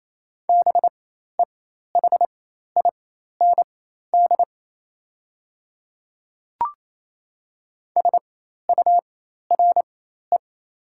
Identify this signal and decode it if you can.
Morse code, heard as a keyed tone, about 28 wpm, spaced as for 13 wpm, 700 Hz. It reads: BEHIND SURE
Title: 4x Character Spacing - Top 500 Words - 28wpm